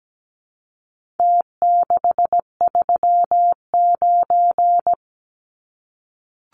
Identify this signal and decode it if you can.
Morse code, heard as a keyed tone, about 17 wpm, 700 Hz. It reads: T639
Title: Numbers - Contest Style - 4-Digit 17wpm